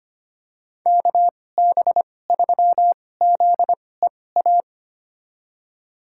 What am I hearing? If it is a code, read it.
Morse code, heard as a keyed tone, about 25 wpm, spaced as for 18 wpm, 700 Hz. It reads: KB3ZEA
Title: USA Call Signs - 6 Characters - 2x Character Spacing - 25wpm